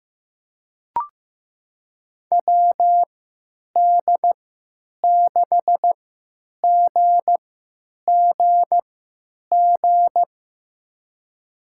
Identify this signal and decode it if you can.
Morse code, heard as a keyed tone, about 15 wpm, spaced as for 9 wpm, 700 Hz. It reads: WD6GGG
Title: USA Call Signs - 6 Characters - 3x Character Spacing - 15wpm